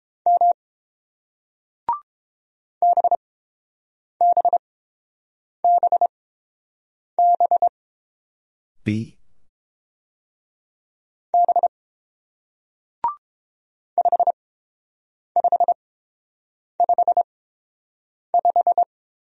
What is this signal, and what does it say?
Morse code, heard as a keyed tone, about 30 wpm, spaced as for 6 wpm, 700 Hz. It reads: M BBBBB 5555